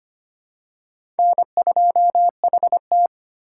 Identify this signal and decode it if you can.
Morse code, heard as a keyed tone, about 25 wpm, 700 Hz. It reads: N2HT
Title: USA Call Signs - 4x Character Spacing - 25wpm